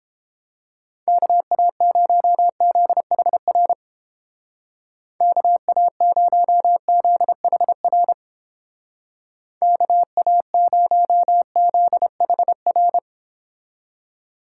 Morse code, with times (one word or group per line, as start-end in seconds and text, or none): 1.1-3.7 KA0ZHR
5.2-8.1 KA0ZHR
9.6-13.0 KA0ZHR